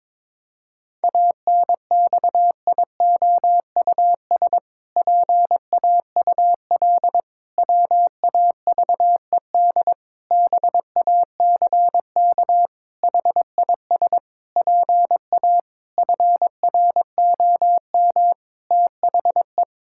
Morse code, heard as a keyed tone, about 22 wpm, 700 Hz. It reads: ANXIOUS PAUL WAVED BACK HIS PA FROM THE
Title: Pangram Challenge v2 - 22wpm